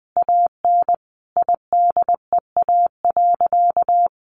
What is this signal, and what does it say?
Morse code, heard as a keyed tone, about 20 wpm, 700 Hz. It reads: AN IDEA.
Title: Additional Easy Phrases 20wpm